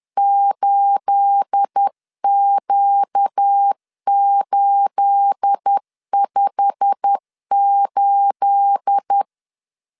Morse code, heard as a keyed tone, about 10 wpm, 800 Hz. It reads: 8Q858
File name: LEZIONI / 9 28 10wpm 2842